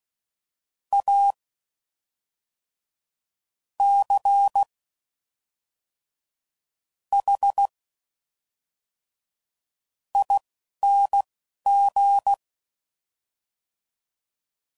Morse code, heard as a keyed tone, about 16 wpm, spaced as for 5 wpm, 800 Hz. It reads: ACHING
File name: Suffixes / Special INGs / Fast INGs